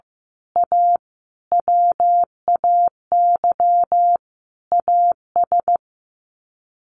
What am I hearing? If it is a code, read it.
Morse code, heard as a keyed tone, about 15 wpm, 700 Hz. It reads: A WAY AS
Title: Sets of 5 Words Encore 15wpm